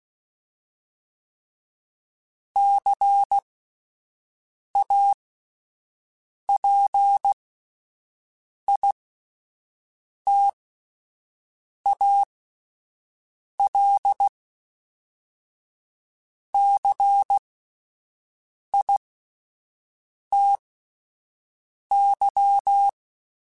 Morse code, heard as a keyed tone, about 16 wpm, spaced as for 5 wpm, 800 Hz. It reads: CAPITALCITY